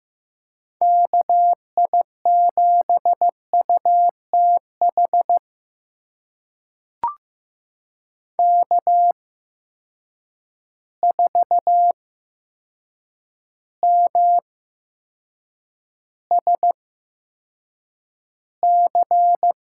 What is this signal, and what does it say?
Morse code, heard as a keyed tone, about 15 wpm, spaced as for 7 wpm, 700 Hz. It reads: KI7UTH K4MSC
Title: USA Call Signs - 8x Character Spacing - 15wpm